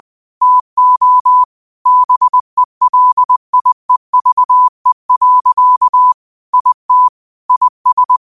Morse code, heard as a keyed tone, about 20 wpm, 1000 Hz. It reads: TO BELIEVE. IT IS